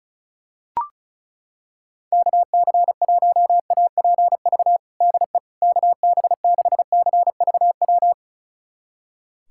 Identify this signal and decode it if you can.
Morse code, heard as a keyed tone, about 36 wpm, 700 Hz. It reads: KC1APV DE KB6CVW
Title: USA Callsign DE Callsign 35wpm